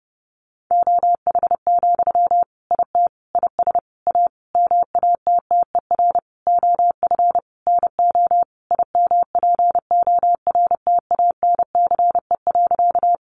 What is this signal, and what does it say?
Morse code, heard as a keyed tone, about 30 wpm, 700 Hz. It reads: OH, IT IS A MATTER OF NO IMPORTANCE.
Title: Sentences from Top 1200 Words 30wpm